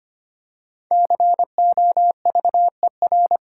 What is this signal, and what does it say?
Morse code, heard as a keyed tone, about 25 wpm, 700 Hz. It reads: COVER
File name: 5 Letter Words - First 3 Letters Given - 25wpm